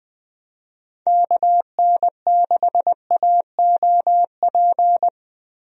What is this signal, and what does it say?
Morse code, heard as a keyed tone, about 20 wpm, 700 Hz. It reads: KN6AOP
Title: USA Call Signs - 6 Characters - Repeated 2x - 20wpm